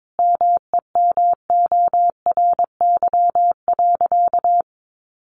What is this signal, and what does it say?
Morse code, heard as a keyed tone, about 22 wpm, 700 Hz. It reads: MEMORY.
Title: Sentences from Top 1300 Words 22wpm